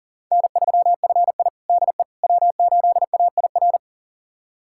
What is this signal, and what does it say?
Morse code, heard as a keyed tone, about 40 wpm, 700 Hz. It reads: N3FI DE W8AIR
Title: USA Callsign DE Callsign 40wpm